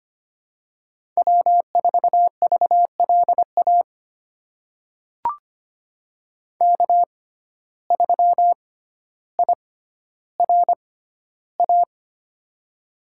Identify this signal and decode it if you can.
Morse code, heard as a keyed tone, about 25 wpm, spaced as for 13 wpm, 700 Hz. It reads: W4VLA K3IRA
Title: USA Call Signs - 5 Characters - 6x Character Spacing - 25wpm